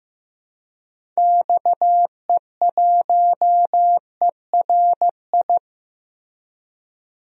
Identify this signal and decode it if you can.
Morse code, heard as a keyed tone, about 16 wpm, 700 Hz. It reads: XE1ERI
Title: DXCC-100 Call Signs - 6 Characters - Repeated 3x - 15wpm